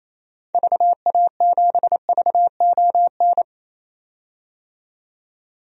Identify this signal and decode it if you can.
Morse code, heard as a keyed tone, about 28 wpm, 700 Hz. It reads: VA7VON
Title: DXCC-100 Call Signs - 6 Characters - 28wpm